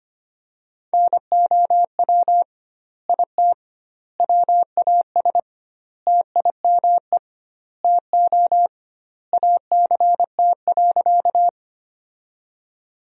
Morse code, heard as a keyed tone, about 25 wpm, 700 Hz. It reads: NOW IT WAS TIME TO ACT.
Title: 2x Word-Spacing - Sentences from Top 200 Words - 25wpm